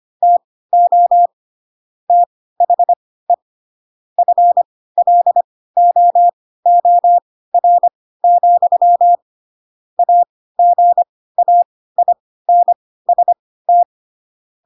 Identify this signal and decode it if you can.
Morse code, heard as a keyed tone, about 25 wpm, spaced as for 16 wpm, 700 Hz. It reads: TO THE FLOOR, AGAINST
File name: Pinocchio / Pinocchio ch1 10 25 ik1hge